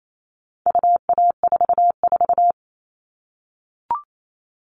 Morse code, heard as a keyed tone, about 28 wpm, 700 Hz. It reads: UA44
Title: All Cut Numbers - 4-Digit 28wpm